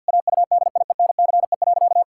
Morse code, heard as a keyed tone, about 50 wpm, 700 Hz. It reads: AUDIENCE.
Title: Sentences from Top 1000 Words - Review Part 5 - 50wpm